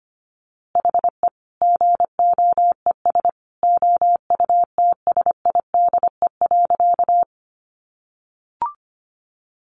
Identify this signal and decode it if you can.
Morse code, heard as a keyed tone, about 25 wpm, 700 Hz. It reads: HE GOES OUTSIDE.
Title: Easy Phrases 25wpm